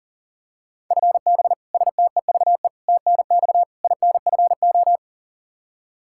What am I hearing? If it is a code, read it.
Morse code, heard as a keyed tone, about 40 wpm, 700 Hz. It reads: FB STEVE TNX INFO
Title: Practice QSO - Exchange 3 - More Info - 40wpm